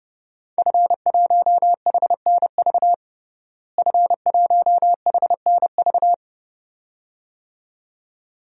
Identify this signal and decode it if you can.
Morse code, heard as a keyed tone, about 32 wpm, 700 Hz. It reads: F1HNV F1HNV
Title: DXCC-100 Call Signs - 5 Characters - Repeated 2x - 30wpm